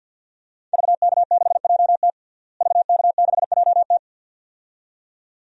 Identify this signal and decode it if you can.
Morse code, heard as a keyed tone, about 49 wpm, 700 Hz. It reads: VK6JT VK6JT
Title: DXCC-100 Call Signs - 5 Characters - Repeated 2x - 50wpm